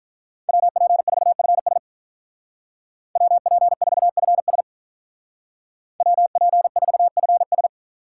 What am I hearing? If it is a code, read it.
Morse code, heard as a keyed tone, about 47 wpm, 700 Hz. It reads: WP4FS WP4FS WP4FS